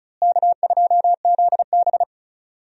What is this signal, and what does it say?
Morse code, heard as a keyed tone, about 36 wpm, 700 Hz. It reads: K2ZB
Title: USA Call Signs - 4 Characters - Repeated 3x - 35wpm